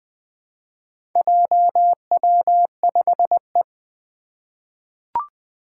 Morse code, heard as a keyed tone, about 20 wpm, 700 Hz. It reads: JW5E
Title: Four Challenging Letter-Number Combinations 20wpm